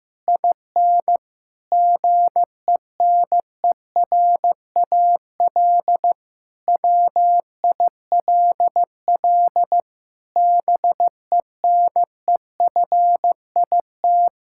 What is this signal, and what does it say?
Morse code, heard as a keyed tone, about 15 wpm, 700 Hz. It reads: IN GENERAL WILL BENEFIT
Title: Sentences from Top 1000 Words - Review Part 6 - 15wpm